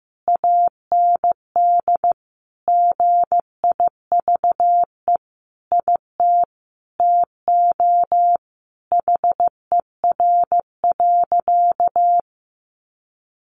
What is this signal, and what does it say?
Morse code, heard as a keyed tone, about 15 wpm, 700 Hz. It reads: AND GIVE IT TO HER.
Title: Sentences from Top 1400 Words 15wpm